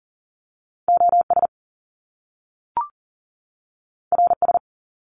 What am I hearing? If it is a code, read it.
Morse code, heard as a keyed tone, about 40 wpm, 700 Hz. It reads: OS RS
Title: Top Two Letter Combinations in Words 40wpm